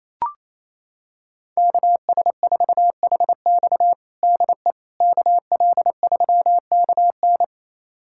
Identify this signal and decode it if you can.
Morse code, heard as a keyed tone, about 28 wpm, 700 Hz. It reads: KS4HX DE KL3KN
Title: USA Callsign DE Callsign - 5 Characters - 28wpm